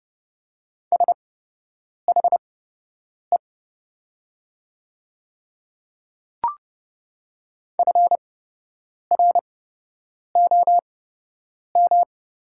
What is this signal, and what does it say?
Morse code, heard as a keyed tone, about 30 wpm, spaced as for 8 wpm, 700 Hz. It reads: SHE FROM
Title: 8x Character Spacing - Top 100 Words - 30wpm